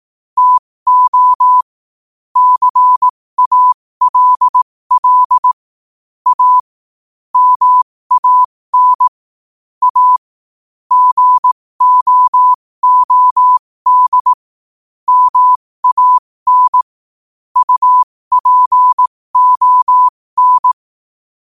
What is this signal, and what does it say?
Morse code, heard as a keyed tone, about 18 wpm, spaced as for 15 wpm, 1000 Hz. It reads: TO CALL A MAN A GOOD MAN UPON
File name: qotdfiles / 15wpm